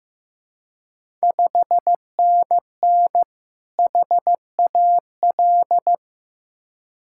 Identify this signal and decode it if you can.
Morse code, heard as a keyed tone, about 15 wpm, 700 Hz. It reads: 5NN HAL
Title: Ontario QSO Party Exchange 15wpm